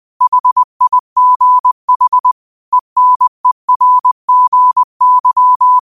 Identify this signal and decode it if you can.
Morse code, heard as a keyed tone, about 20 wpm, 1000 Hz. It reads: HIGH ENERGY